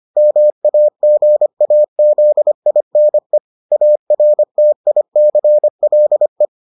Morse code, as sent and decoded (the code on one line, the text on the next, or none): -- .- --. .- --.. .. -. / .- .-. - .. -.-. .-.. .
MAGAZINE ARTICLE